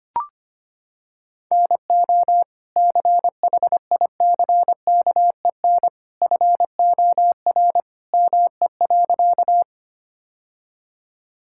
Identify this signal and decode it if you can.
Morse code, heard as a keyed tone, about 25 wpm, 700 Hz. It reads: NO CHICKEN FOR ME.
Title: Sentences from Top 1000 Words - Review Part 5 - 25wpm